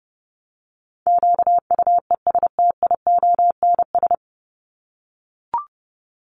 Morse code, heard as a keyed tone, about 30 wpm, 700 Hz. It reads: QUESTIONS
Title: ES:S Suffix - Top 100 Words with Base Words 30wpm